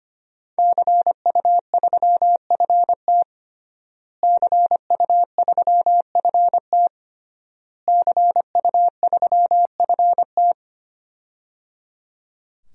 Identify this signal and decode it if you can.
Morse code, heard as a keyed tone, about 25 wpm, 700 Hz. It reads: CU3FT CU3FT CU3FT
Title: DXCC-100 Call Signs - 5 Characters - Repeated 3x - 25wpm